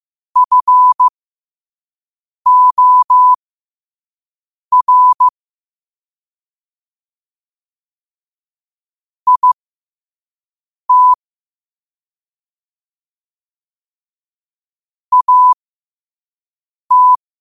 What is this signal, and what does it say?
Morse code, heard as a keyed tone, about 15 wpm, spaced as for 5 wpm, 1000 Hz. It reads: FOR IT AT
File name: qotdfiles / 5wpm